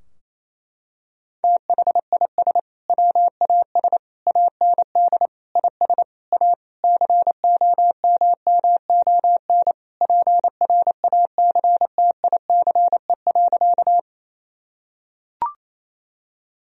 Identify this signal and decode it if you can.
Morse code, heard as a keyed tone, about 28 wpm, 700 Hz. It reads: THIS WAS AND IS A COMMON PRACTICE.
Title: Sentences from Top 1000 Words - Review Part 4 - 28wpm